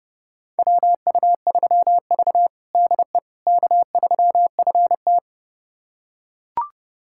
Morse code, heard as a keyed tone, about 32 wpm, 700 Hz. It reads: WU3V DE K3FT
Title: USA Callsign DE Callsign - 4 Characters - 30wpm